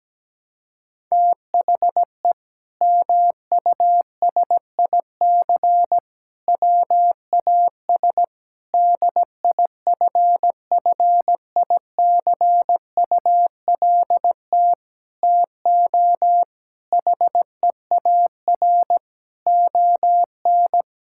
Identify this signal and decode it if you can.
Morse code, heard as a keyed tone, about 17 wpm, 700 Hz. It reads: THE MUSIC WAS DIFFICULT TO HEAR ON
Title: Sentences from Top 1000 Words - Review Part 6 - 17wpm